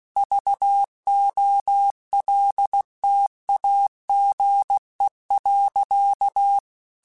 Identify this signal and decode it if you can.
Morse code, heard as a keyed tone, about 16 wpm, 800 Hz. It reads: VOLTAGE.
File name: Technical Subjects / Technical Subjects-mp3s / XL @15